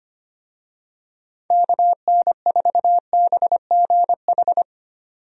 Morse code, heard as a keyed tone, about 25 wpm, 700 Hz. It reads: KN4BGH
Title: USA Call Signs - 6 Characters - 25wpm